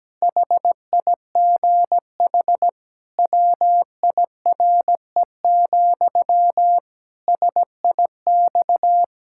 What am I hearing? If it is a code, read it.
Morse code, heard as a keyed tone, about 17 wpm, 700 Hz. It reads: HIGH WIRE, SIX
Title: Pangram Challenge v2 - 17wpm